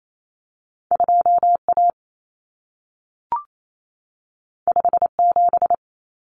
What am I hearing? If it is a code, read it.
Morse code, heard as a keyed tone, about 28 wpm, 700 Hz. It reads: 2A 57
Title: Common Cut Numbers - 2-Digit 28wpm